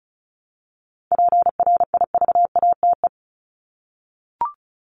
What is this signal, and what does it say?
Morse code, heard as a keyed tone, about 34 wpm, 700 Hz. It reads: PRIVATE
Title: Top 751-1000 Words 35wpm